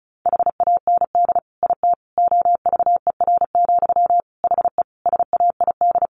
Morse code, heard as a keyed tone, about 34 wpm, 700 Hz. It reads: HAND IT OVER, HE SAID
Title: Sentences from Top 1500 Words 35wpm